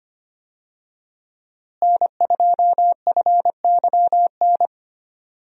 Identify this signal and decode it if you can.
Morse code, heard as a keyed tone, about 25 wpm, 700 Hz. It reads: N2FYN
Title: USA Call Signs - 5 Characters - Repeated 2x - 25wpm